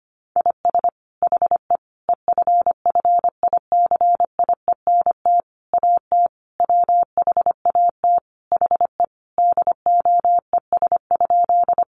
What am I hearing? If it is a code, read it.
Morse code, heard as a keyed tone, about 25 wpm, 700 Hz. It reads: IS HE EFFICIENT AT WHAT HE DOES?
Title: Sentences from Top 1300 Words 25wpm